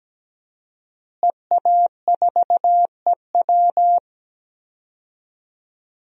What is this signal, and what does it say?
Morse code, heard as a keyed tone, about 17 wpm, 700 Hz. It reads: EA4EW
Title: DXCC-100 Call Signs - 5 Characters - Repeated 2x - 17wpm